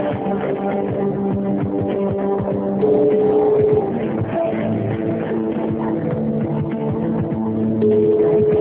Ringback tone (Europe, 425 Hz), with noise.